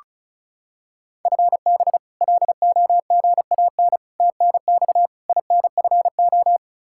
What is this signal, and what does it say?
Morse code, heard as a keyed tone, about 34 wpm, 700 Hz. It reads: FB LOGAN TNX INFO
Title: K1USN SST QSO Practice 35wpm